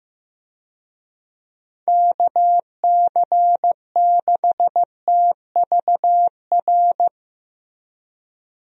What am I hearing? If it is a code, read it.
Morse code, heard as a keyed tone, about 15 wpm, 700 Hz. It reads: KC6TVR
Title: USA Call Signs - 6 Characters - Repeated 3x - 15wpm